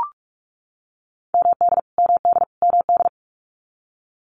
Morse code, heard as a keyed tone, about 44 wpm, 700 Hz. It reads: MD MD MD
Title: US States Repeated 3x 45wpm